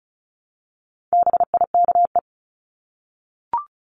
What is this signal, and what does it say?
Morse code, heard as a keyed tone, about 34 wpm, 700 Hz. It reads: BIKE
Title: Top 1751-2000 Words 35wpm